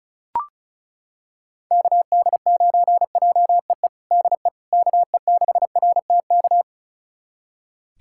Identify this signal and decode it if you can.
Morse code, heard as a keyed tone, about 36 wpm, 700 Hz. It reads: KD9JEE DE KE6RTK
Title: USA Callsign DE Callsign - 6 Characters - 35wpm